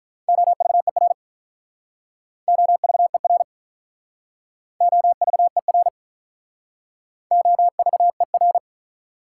Morse code, heard as a keyed tone, about 46 wpm, 700 Hz. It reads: OVER OVER OVER OVER